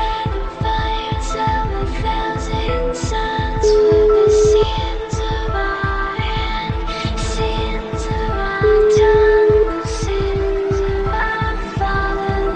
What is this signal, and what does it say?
Ringback tone (Europe, 425 Hz).